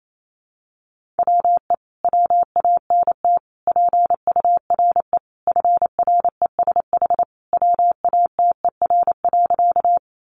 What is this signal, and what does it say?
Morse code, heard as a keyed tone, about 28 wpm, 700 Hz. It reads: WE WANT PURE FRESH WATER.
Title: Sentences from Top 1500 Words 28wpm